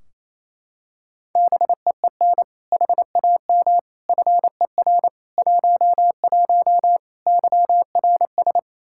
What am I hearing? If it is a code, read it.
Morse code, heard as a keyed tone, about 28 wpm, 700 Hz. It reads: BEEN HAM FER 11 YRS
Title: Practice QSO - Exchange 3 - More Info - 28wpm